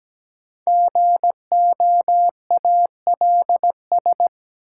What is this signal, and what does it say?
Morse code, heard as a keyed tone, about 17 wpm, 700 Hz. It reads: GOALS
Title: Sentences from Top 1000 Words - Review Part 6 - New Word Drill - 17wpm